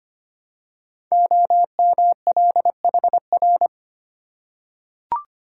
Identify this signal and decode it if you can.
Morse code, heard as a keyed tone, about 25 wpm, 700 Hz. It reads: OMLHR
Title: Code Groups - 5 Letters - 25wpm